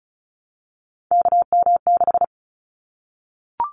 Morse code, heard as a keyed tone, about 36 wpm, 700 Hz. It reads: KM6
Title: Callsign Prefixes for Top 25 DXCC - 35wpm